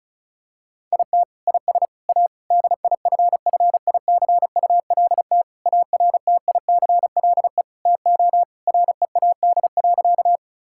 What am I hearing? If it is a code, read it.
Morse code, heard as a keyed tone, about 36 wpm, 700 Hz. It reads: IT IS A DIFFICULT ARTICLE TO READ.
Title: Sentences from Top 1000 Words - Review Part 5 - 35wpm